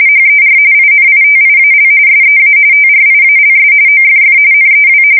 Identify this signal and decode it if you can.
RTTY, about 45.45 baud. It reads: HER INITIAL SEA TRIALS OFF NEW
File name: fsk170 / USS-Missouri